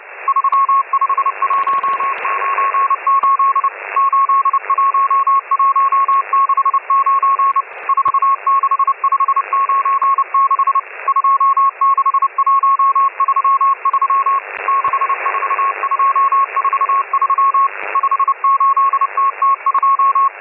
Morse code, heard as a keyed tone, about 30 wpm, 1100 Hz, with radio noise. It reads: 35198 81169 46596 26133 64243 58TT2
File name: M12- -1010utc